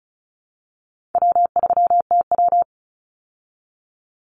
Morse code, heard as a keyed tone, about 34 wpm, 700 Hz. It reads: W3TW
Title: Easy Call Signs Repeated 3x - 35wpm